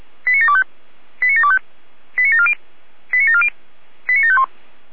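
Five-tone selective call (ZVEI), heard as quick runs of five tones, tones of about 70 ms. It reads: 89725 89725 89740 89740 89751
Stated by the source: ZVEI five-tone selective call received by SDR.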